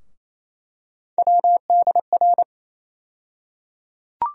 Morse code, heard as a keyed tone, about 28 wpm, 700 Hz. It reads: WDR
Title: USA Call Sign Postfixes - 3 Letters 28wpm